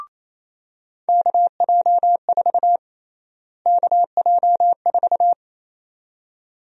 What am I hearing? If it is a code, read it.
Morse code, heard as a keyed tone, about 28 wpm, 700 Hz. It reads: KJ4 KJ4
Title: USA Call Sign Prefixes - Repeated 2x - 28wpm